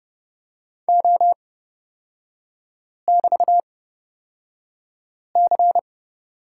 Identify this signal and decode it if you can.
Morse code, heard as a keyed tone, about 30 wpm, spaced as for 5 wpm, 700 Hz. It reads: O<BT>C
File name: Lesson 37e - Letter x - Three Characters 30wpm